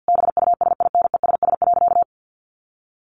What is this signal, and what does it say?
Morse code, heard as a keyed tone, about 50 wpm, 700 Hz. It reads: BUSINESS.
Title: Sentences from Top 1400 Words 50wpm